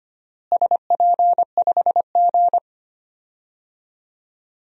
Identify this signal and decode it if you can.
Morse code, heard as a keyed tone, about 25 wpm, 700 Hz. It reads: SP5G
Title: DXCC-100 Call Signs - 4 Characters - Repeated 3x - 25wpm